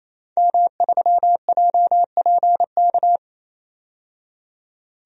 Morse code, heard as a keyed tone, about 28 wpm, 700 Hz. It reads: M3JPK
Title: DXCC-100 Call Signs - 5 Characters - 28wpm